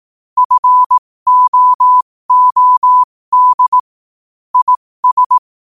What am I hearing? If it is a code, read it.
Morse code, heard as a keyed tone, about 18 wpm, spaced as for 15 wpm, 1000 Hz. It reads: FOOD IS
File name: qotdfiles / 15wpm